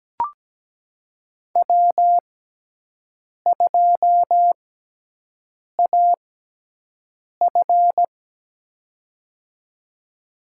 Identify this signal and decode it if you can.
Morse code, heard as a keyed tone, about 17 wpm, spaced as for 6 wpm, 700 Hz. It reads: W2AF
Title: USA Call Signs - 4 Characters - 6x Character Spacing - 17wpm